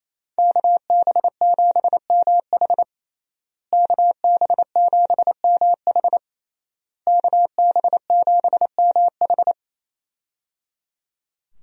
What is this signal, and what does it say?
Morse code, heard as a keyed tone, about 28 wpm, 700 Hz. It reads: KB7MH KB7MH KB7MH
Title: USA Call Signs - 5 Characters - Repeated 3x - 28wpm